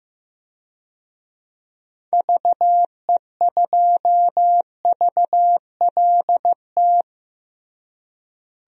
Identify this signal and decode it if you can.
Morse code, heard as a keyed tone, about 15 wpm, 700 Hz. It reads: VE2VLT
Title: DXCC-100 Call Signs - 6 Characters - Repeated 2x - 15wpm